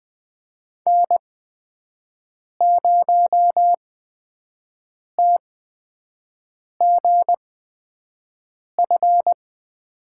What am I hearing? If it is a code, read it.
Morse code, heard as a keyed tone, about 20 wpm, spaced as for 5 wpm, 700 Hz. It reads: N0TGF
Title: USA Call Signs - 5 Characters - 8x Character Spacing - 20wpm